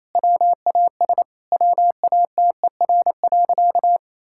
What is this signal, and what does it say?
Morse code, heard as a keyed tone, about 28 wpm, 700 Hz. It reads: WAS WATER.
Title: Sentences from Top 1600 Words 28wpm